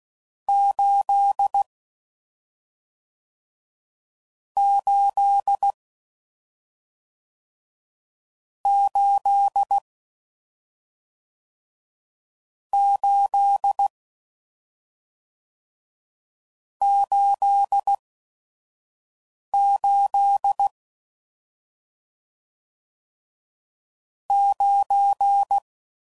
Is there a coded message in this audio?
Yes: Morse code from 0.5 to 25.6 s, heard as a keyed tone, about 16 wpm, 800 Hz. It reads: 8 8 8 8 8 8 9